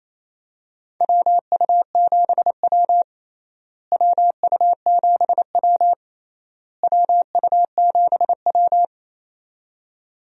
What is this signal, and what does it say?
Morse code, heard as a keyed tone, about 28 wpm, 700 Hz. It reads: WU7W WU7W WU7W
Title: USA Call Signs - 4 Characters - Repeated 3x - 28wpm